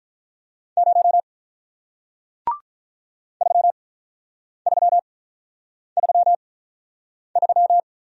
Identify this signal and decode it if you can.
Morse code, heard as a keyed tone, about 46 wpm, 700 Hz. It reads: 0 3 3 3 3